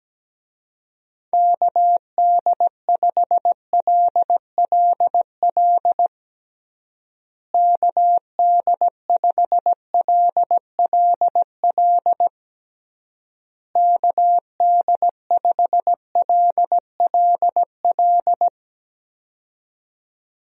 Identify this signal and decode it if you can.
Morse code, heard as a keyed tone, about 17 wpm, 700 Hz. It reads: KD5LLL KD5LLL KD5LLL
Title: USA Call Signs - 6 Characters - Repeated 3x - 17wpm